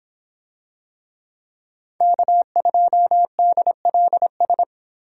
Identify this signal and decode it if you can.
Morse code, heard as a keyed tone, about 26 wpm, 700 Hz. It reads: K2DLS